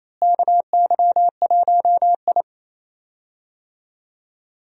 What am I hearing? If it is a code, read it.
Morse code, heard as a keyed tone, about 28 wpm, 700 Hz. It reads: KY1I
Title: USA Call Signs - 4 Characters - Repeated 2x - 28wpm